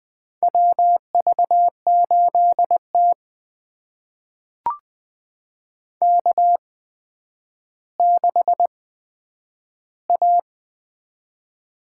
Morse code, heard as a keyed tone, about 20 wpm, spaced as for 10 wpm, 700 Hz. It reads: WV8T K6A
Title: USA Call Signs - 4 Characters - 8x Character Spacing - 20wpm